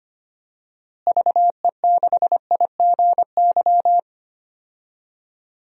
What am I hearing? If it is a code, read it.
Morse code, heard as a keyed tone, about 25 wpm, 700 Hz. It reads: VE6IGY
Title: DXCC-100 Call Signs - 6 Characters - Repeated 2x - 25wpm